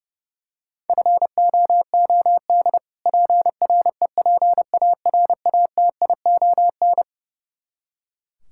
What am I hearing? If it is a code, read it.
Morse code, heard as a keyed tone, about 32 wpm, 700 Hz. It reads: FOOD PREPARATION
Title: K1USN SST QSO Practice 30wpm